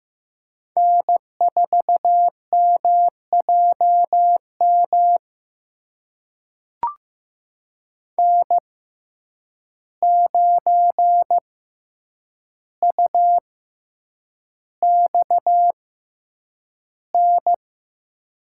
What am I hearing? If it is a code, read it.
Morse code, heard as a keyed tone, about 15 wpm, spaced as for 8 wpm, 700 Hz. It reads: N4MJM N9UXN
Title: USA Call Signs - 6x Character Spacing - 15wpm